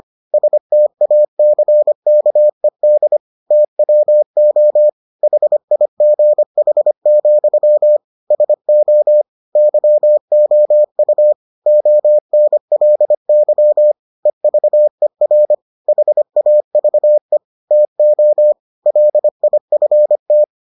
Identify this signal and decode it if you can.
Morse code, heard as a keyed tone, about 25 wpm, 600 Hz. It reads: STACKED TWO HIGH, SO YOU ONLY EVER HAVE TO LIFT